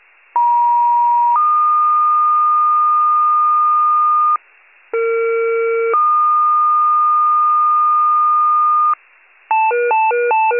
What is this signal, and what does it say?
Two-tone page, heard as a one-second tone, then a three-second tone, about 950 then 1220 Hz, about 460 then 1220 Hz. It is heard from 0.3 to 8.9 s.
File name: pager / 83 RESCUE / wcalert